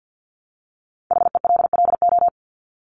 Morse code, heard as a keyed tone, about 50 wpm, 700 Hz. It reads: HELLO
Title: Top 1001-1250 Words 50wpm